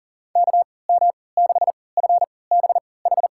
Morse code, heard as a keyed tone, about 40 wpm, spaced as for 23 wpm, 700 Hz. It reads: KM6FBH
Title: USA Call Signs - 6 Characters - 3x Character Spacing - 40wpm